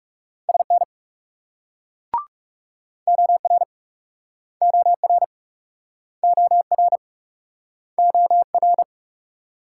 Morse code, heard as a keyed tone, about 48 wpm, 700 Hz. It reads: SN OR OR OR OR